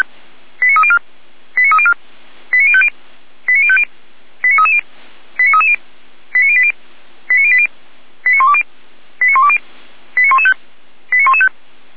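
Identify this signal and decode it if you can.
Five-tone selective call (ZVEI), heard as quick runs of five tones, tones of about 70 ms. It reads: 89304 89304 89050 89050 89330 89330 89080 89080 89120 89120 89115 89115
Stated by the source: ZVEI five-tone selective call received by SDR.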